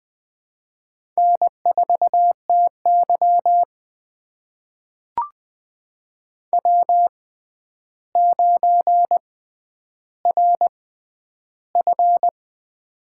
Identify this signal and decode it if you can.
Morse code, heard as a keyed tone, about 21 wpm, spaced as for 10 wpm, 700 Hz. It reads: N4TY W9RF
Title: USA Call Signs - 4 Characters - 6x Character Spacing - 20wpm